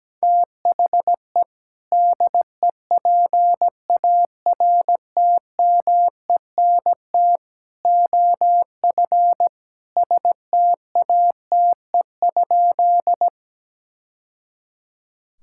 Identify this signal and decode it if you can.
Morse code, heard as a keyed tone, about 17 wpm, 700 Hz. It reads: THE DEPARTMENT OF STATE?
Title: Sentences from Top 1000 Words - Review Part 5 - 17wpm